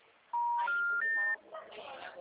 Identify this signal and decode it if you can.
Special information tone, with speech.